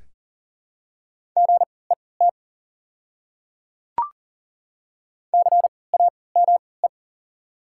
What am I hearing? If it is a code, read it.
Morse code, heard as a keyed tone, about 38 wpm, spaced as for 22 wpm, 700 Hz. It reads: GET CAME
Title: 3x Character Spacing - Top 200 Words - 40wpm